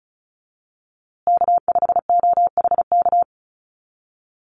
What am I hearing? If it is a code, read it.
Morse code, heard as a keyed tone, about 34 wpm, 700 Hz. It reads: K5OHK
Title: USA Call Signs Repeated 2x 35wpm